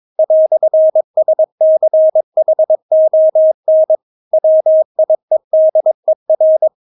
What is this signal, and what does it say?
Morse code, heard as a keyed tone, about 22 wpm, 630 Hz. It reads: "SCHON WIEDER
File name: Witz3 22wpm